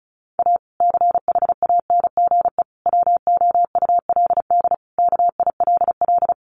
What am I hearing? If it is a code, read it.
Morse code, heard as a keyed tone, about 34 wpm, 700 Hz. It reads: A CHANGE WOULD KILL
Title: Sentences from Top 1400 Words 35wpm